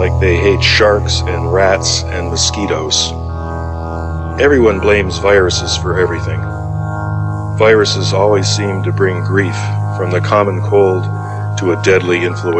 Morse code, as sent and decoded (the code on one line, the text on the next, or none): none